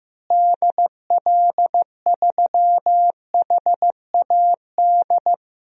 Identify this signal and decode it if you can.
Morse code, heard as a keyed tone, about 15 wpm, 700 Hz. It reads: DL3HAD
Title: DXCC-100 Call Signs - 6 Characters - Repeated 2x - 15wpm